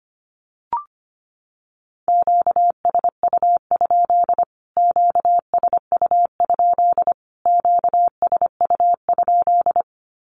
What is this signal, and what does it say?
Morse code, heard as a keyed tone, about 25 wpm, 700 Hz. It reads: QSU? QSU? QSU?
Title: Q-Codes Repeated 3x 25wpm